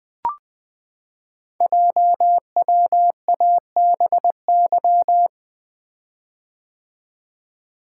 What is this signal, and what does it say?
Morse code, heard as a keyed tone, about 20 wpm, 700 Hz. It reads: JWABY
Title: Code Groups - 5 Letters - 20wpm